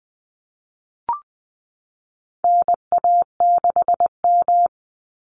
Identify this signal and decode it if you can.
Morse code, heard as a keyed tone, about 20 wpm, 700 Hz. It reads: NA6M
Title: Call Signs Repeated 3x - 20wpm